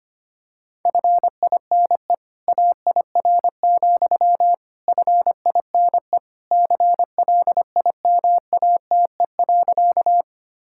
Morse code, heard as a keyed tone, about 25 wpm, 700 Hz. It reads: FINE AIR, FINE CLIMATE.
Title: Sentences from Top 1600 Words 25wpm